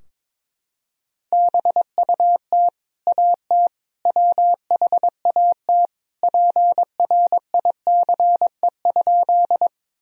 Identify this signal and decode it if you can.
Morse code, heard as a keyed tone, about 21 wpm, 700 Hz. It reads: BUT AT WHAT PRICE?
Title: Sentences from Top 1000 Words - Review Part 4 - 22wpm